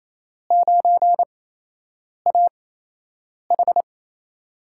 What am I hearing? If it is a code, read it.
Morse code, heard as a keyed tone, about 28 wpm, spaced as for 8 wpm, 700 Hz. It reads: 9AH
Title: USA Call Signs - 4 Characters - 8x Character Spacing - 28wpm